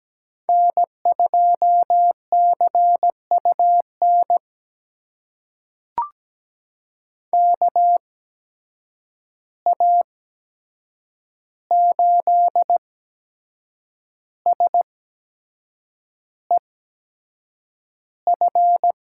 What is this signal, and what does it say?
Morse code, heard as a keyed tone, about 17 wpm, spaced as for 7 wpm, 700 Hz. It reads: N2CUN KA8SEF
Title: USA Call Signs - 8x Character Spacing - 17wpm